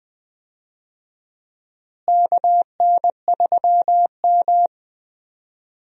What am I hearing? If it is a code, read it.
Morse code, heard as a keyed tone, about 20 wpm, 700 Hz. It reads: KN3M
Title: USA Call Signs - 4 Characters - Repeated 2x - 20wpm